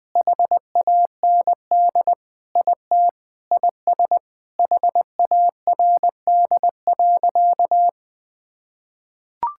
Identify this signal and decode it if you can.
Morse code, heard as a keyed tone, about 20 wpm, 700 Hz. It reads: HAND IT IS HARD.
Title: Sentences with 4-Word Phrases - 20wpm